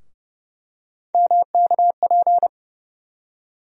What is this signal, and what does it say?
Morse code, heard as a keyed tone, about 30 wpm, 700 Hz. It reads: MKP
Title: USA Call Sign Postfixes - 3 Letters 30wpm